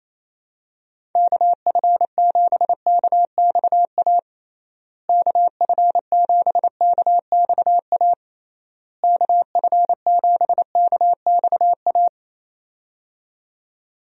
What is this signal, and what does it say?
Morse code, heard as a keyed tone, about 28 wpm, 700 Hz. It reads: KF7KXA KF7KXA KF7KXA
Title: USA Call Signs - 6 Characters - Repeated 3x - 28wpm